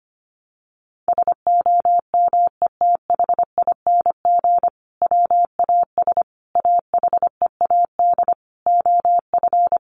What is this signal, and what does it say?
Morse code, heard as a keyed tone, about 25 wpm, 700 Hz. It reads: SOMETHING WAS AHEAD OF
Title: Sentences from Top 1100 Words 25wpm